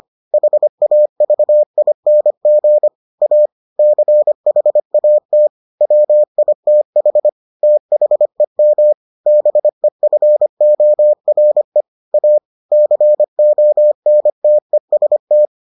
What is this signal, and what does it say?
Morse code, heard as a keyed tone, about 25 wpm, 600 Hz. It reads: HAVING A CHAT WITH THEM BEFORE A CONTEST